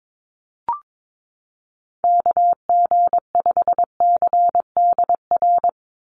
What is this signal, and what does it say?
Morse code, heard as a keyed tone, about 22 wpm, 700 Hz. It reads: KG5CDR
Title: USA Call Signs 22wpm